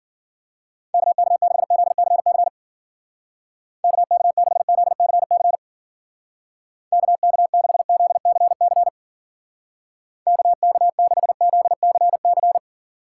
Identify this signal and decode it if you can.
Morse code, heard as a keyed tone, about 52 wpm, 700 Hz. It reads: KK6ZCC KK6ZCC KK6ZCC KK6ZCC